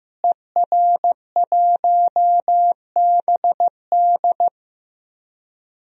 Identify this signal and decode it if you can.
Morse code, heard as a keyed tone, about 15 wpm, 700 Hz. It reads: ER1BD
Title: DXCC-100 Call Signs - 5 Characters - Repeated 3x - 15wpm